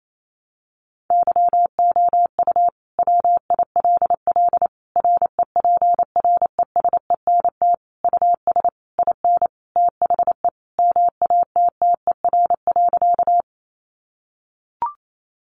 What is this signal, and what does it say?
Morse code, heard as a keyed tone, about 28 wpm, 700 Hz. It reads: YOU WILL REPRESENT US IN THE MATTER.
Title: Sentences from Top 1200 Words 28wpm